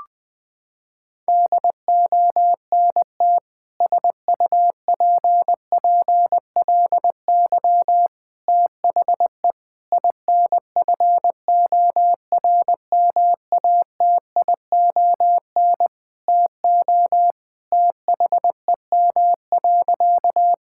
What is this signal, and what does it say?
Morse code, heard as a keyed tone, about 20 wpm, 700 Hz. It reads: DONT SUPPLY THE INFORMATION TO THEM.
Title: Sentences from Top 1000 Words - Review Part 5 - 20wpm